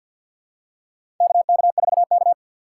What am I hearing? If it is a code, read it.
Morse code, heard as a keyed tone, about 52 wpm, 700 Hz. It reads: KK4K
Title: USA Call Signs - 4 Characters - 3x Character Spacing - 50wpm